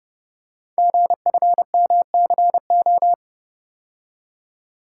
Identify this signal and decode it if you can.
Morse code, heard as a keyed tone, about 30 wpm, 700 Hz. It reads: GFMCO
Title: Code Groups - 5 Letters - 30wpm